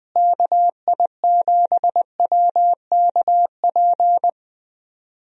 Morse code, heard as a keyed tone, about 20 wpm, 700 Hz. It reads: KI7WKP
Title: USA Call Signs - 6 Characters - Repeated 2x - 20wpm